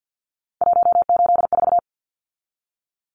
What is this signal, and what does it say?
Morse code, heard as a keyed tone, about 50 wpm, 700 Hz. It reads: 184
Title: Numbers - 3-Digit 50wpm